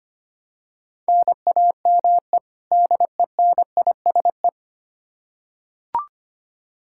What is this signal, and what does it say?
Morse code, heard as a keyed tone, about 25 wpm, 700 Hz. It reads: NAME DENISE
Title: Practice QSO - Exchange 1 - Beginning - 25wpm